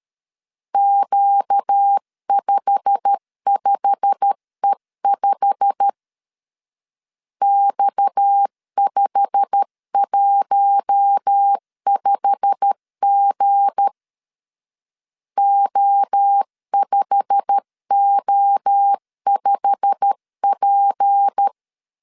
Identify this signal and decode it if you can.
Morse code, heard as a keyed tone, about 13 wpm, 800 Hz. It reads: Q55E5 X515G O5O5P
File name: LEZIONI / 8 YJ45 / 8 YJ45 12wpm 2151